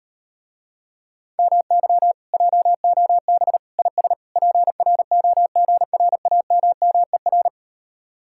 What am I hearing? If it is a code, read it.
Morse code, heard as a keyed tone, about 38 wpm, 700 Hz. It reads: MY JOB IS PROGRAMMER